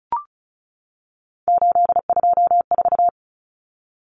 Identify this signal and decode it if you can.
Morse code, heard as a keyed tone, about 34 wpm, 700 Hz. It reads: 824
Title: Common Cut Numbers - 3-Digit 35wpm